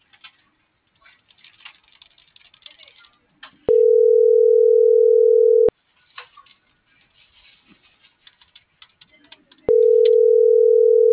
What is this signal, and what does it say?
Ringback tone (North America), 440 + 480 Hz.